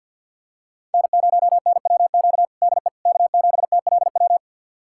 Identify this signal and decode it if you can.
Morse code, heard as a keyed tone, about 50 wpm, 690 Hz. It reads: N0NWQ DE K7TLW
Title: USA Callsign DE Callsign - 5 Characters - 50wpm